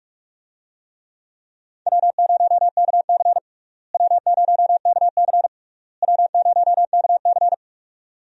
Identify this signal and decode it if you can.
Morse code, heard as a keyed tone, about 46 wpm, 700 Hz. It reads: W0KC W0KC W0KC
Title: USA Call Signs - 4 Characters - Repeated 3x - 45wpm